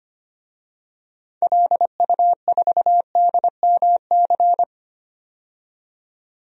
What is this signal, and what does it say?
Morse code, heard as a keyed tone, about 25 wpm, 700 Hz. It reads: LU4DMC
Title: DXCC-100 Call Signs - 6 Characters - 25wpm